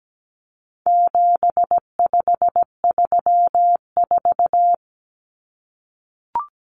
Morse code, heard as a keyed tone, about 17 wpm, 700 Hz. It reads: 7534
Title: Common Cut Numbers - 4-Digit 17wpm